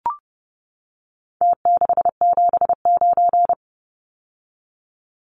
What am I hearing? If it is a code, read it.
Morse code, heard as a keyed tone, about 30 wpm, 700 Hz. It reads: T679
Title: Numbers - Contest Style - 4-Digit 30wpm